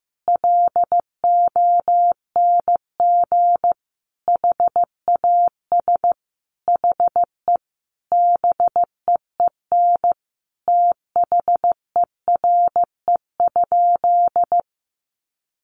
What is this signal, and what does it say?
Morse code, heard as a keyed tone, about 15 wpm, 700 Hz. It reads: LONG HAS HE BEEN THERE?
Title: Sentences from Top 100 Words 15wpm